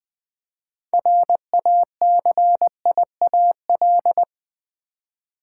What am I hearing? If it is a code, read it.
Morse code, heard as a keyed tone, about 20 wpm, 700 Hz. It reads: RACIAL
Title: 6 Letter Words - First 3 Letters Given - 20wpm